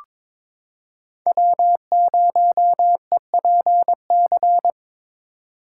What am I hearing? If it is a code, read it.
Morse code, heard as a keyed tone, about 22 wpm, 700 Hz. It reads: W0EPC
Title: K1USN SST QSO Practice - 22wpm